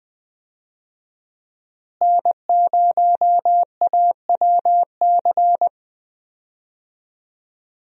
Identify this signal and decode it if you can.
Morse code, heard as a keyed tone, about 21 wpm, 700 Hz. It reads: N0AWC
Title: USA Call Signs - 5 Characters - Repeated 2x - 20wpm